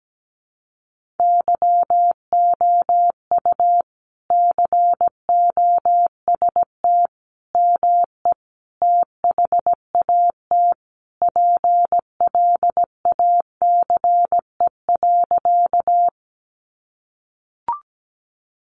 Morse code, heard as a keyed tone, about 17 wpm, 700 Hz. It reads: YOU COST ME THAT PLACE.
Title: Sentences from Top 800 Words 17wpm